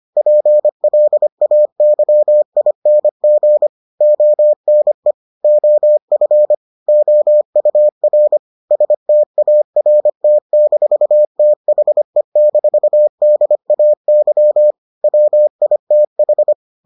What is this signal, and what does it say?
Morse code, heard as a keyed tone, about 25 wpm, 600 Hz. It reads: PLAYING ONE OF OUR START-THE-DAY WITH